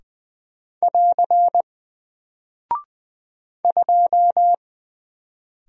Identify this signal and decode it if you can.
Morse code, heard as a keyed tone, about 20 wpm, 700 Hz. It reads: <AR> 2
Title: Single Character (Letter:Number:Punctuation:Prosign) Alt 20wpm